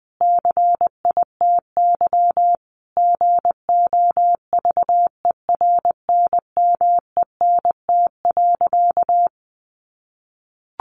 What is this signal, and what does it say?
Morse code, heard as a keyed tone, about 20 wpm, 700 Hz. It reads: CITY GOVERNMENT.
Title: Sentences from Top 1400 Words 20wpm